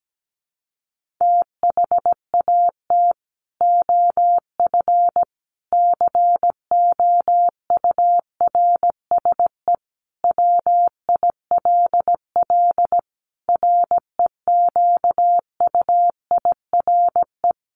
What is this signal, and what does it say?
Morse code, heard as a keyed tone, about 17 wpm, 700 Hz. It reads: THAT OF COURSE WILL REQUIRE
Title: Sentences from Top 1500 Words 17wpm